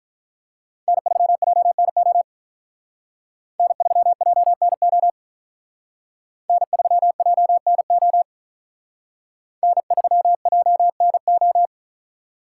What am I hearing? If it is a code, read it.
Morse code, heard as a keyed tone, about 43 wpm, 700 Hz. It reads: N3JNO N3JNO N3JNO N3JNO
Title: Speed Racing - USA Call Signs 35wpm